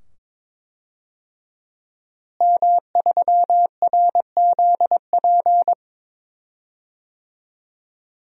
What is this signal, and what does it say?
Morse code, heard as a keyed tone, about 22 wpm, 700 Hz. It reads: M3RZP
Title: DXCC-100 Call Signs - 5 Characters - Repeated 2x - 22wpm